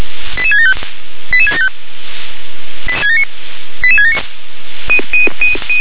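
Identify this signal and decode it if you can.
Five-tone selective call (ZVEI), heard as quick runs of five tones, tones of about 70 ms. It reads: 88675 88675 88679
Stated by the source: ZVEI five-tone selective call 88773 received by SDR.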